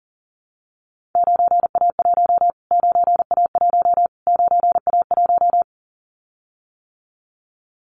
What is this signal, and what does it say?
Morse code, heard as a keyed tone, about 40 wpm, 700 Hz. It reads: 9A1 9A1 9A1
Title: Callsign Prefixes for Top 25 DXCC - Repeated 3x - 40wpm